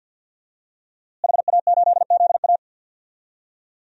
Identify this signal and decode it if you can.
Morse code, heard as a keyed tone, about 47 wpm, 700 Hz. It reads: HA8ZA
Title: DXCC-100 Call Signs - 5 Characters - Repeated 3x - 50wpm